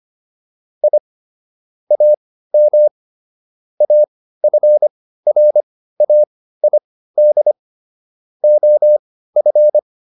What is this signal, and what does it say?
Morse code, heard as a keyed tone, about 25 wpm, spaced as for 15 wpm, 600 Hz. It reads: I AM AFRAID OF